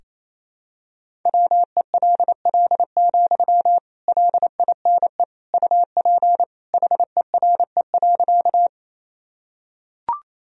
Morse code, heard as a keyed tone, about 28 wpm, 700 Hz. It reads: WELL, LINE UP HERE.
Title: Sentences from Top 1000 Words - Review Part 1 - 28wpm